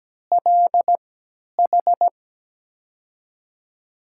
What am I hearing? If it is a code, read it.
Morse code, heard as a keyed tone, about 17 wpm, spaced as for 10 wpm, 700 Hz. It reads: LH
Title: USA Call Signs - 6 Characters - 3x Character Spacing - 17wpm